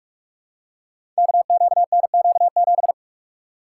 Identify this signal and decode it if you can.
Morse code, heard as a keyed tone, about 44 wpm, 700 Hz. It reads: KQNQ7
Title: Code Groups - 5 Letter:Numbers - 45wpm